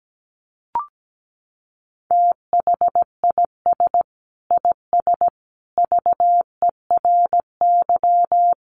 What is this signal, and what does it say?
Morse code, heard as a keyed tone, about 17 wpm, 700 Hz. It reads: THIS IS VERY
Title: Sentences from Top 1200 Words 17wpm